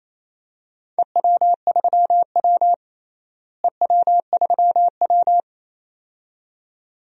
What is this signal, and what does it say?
Morse code, heard as a keyed tone, about 27 wpm, 700 Hz. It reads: EW3W EW3W
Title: DXCC-100 Call Signs - 4 Characters - Repeated 2x - 28wpm